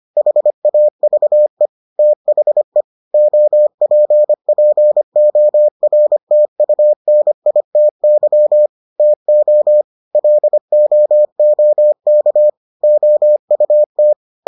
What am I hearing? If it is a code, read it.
Morse code, heard as a keyed tone, about 25 wpm, 600 Hz. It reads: HAVE THE OPPORTUNITY TO LOOK OUT